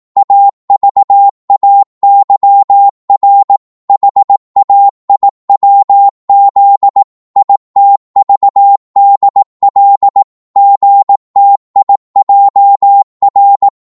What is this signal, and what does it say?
Morse code, heard as a keyed tone, about 18 wpm, 800 Hz. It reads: AVAYR HAIWZ ITVDL GTIJR
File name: Morse / ltr-18wpm